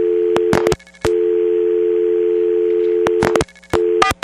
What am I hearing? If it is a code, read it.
Dial tone (North America), 350 + 440 Hz.
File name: TurtleCreekEWSD